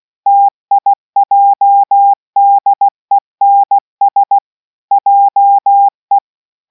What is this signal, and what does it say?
Morse code, heard as a keyed tone, about 16 wpm, 800 Hz. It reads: TIJDENS JE